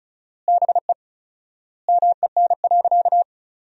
Morse code, heard as a keyed tone, about 36 wpm, 700 Hz. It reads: BE MEN.
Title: 4x Word-Spacing - Sentences from Top 300 Words - 35wpm